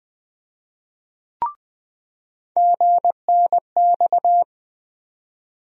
Morse code, heard as a keyed tone, about 20 wpm, 700 Hz. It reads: GNX
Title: USA Call Sign Postfixes - 3 Letters 20wpm